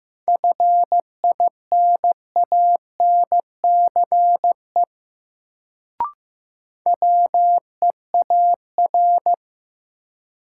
Morse code, heard as a keyed tone, about 15 wpm, 700 Hz. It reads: FINANCE WEAR
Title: Sentences from Top 1500 Words - New Word Drill Alt - 15wpm